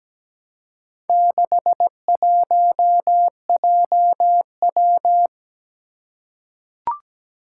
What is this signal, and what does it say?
Morse code, heard as a keyed tone, about 17 wpm, 700 Hz. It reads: 61JW
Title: Four Challenging Letter-Number Combinations 17wpm